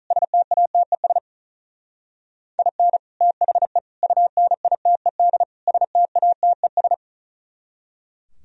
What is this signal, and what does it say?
Morse code, heard as a keyed tone, about 46 wpm, 700 Hz. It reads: STATES IN THE UNITED STATES